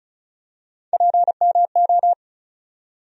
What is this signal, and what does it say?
Morse code, heard as a keyed tone, about 36 wpm, 700 Hz. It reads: PMO
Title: USA Call Sign Postfixes - 3 Letters 35wpm